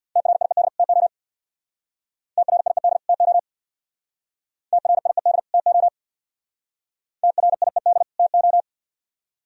Morse code, heard as a keyed tone, about 62 wpm, 700 Hz. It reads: TRIED TO TRIED TO TRIED TO TRIED TO
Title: Speed-Racing - Sets of 2 Words 50wpm